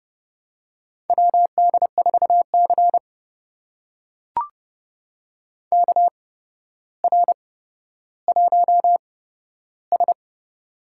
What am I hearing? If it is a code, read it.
Morse code, heard as a keyed tone, about 30 wpm, spaced as for 13 wpm, 700 Hz. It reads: WD4C KR1S
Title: USA Call Signs - 4 Characters - 8x Character Spacing - 30wpm